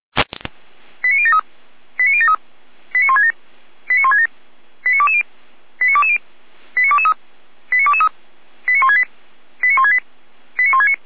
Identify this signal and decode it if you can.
Five-tone selective call (ZVEI), heard as quick runs of five tones, tones of about 70 ms. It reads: 89073 89073 89167 89167 89220 89220 89223 89223 89168 89168 89169